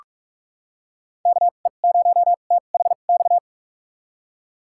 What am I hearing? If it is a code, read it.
Morse code, heard as a keyed tone, about 46 wpm, spaced as for 33 wpm, 700 Hz. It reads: KE0THX
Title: USA Call Signs - 2x Character Spacing - 45wpm